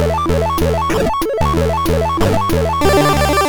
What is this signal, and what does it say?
There is no code.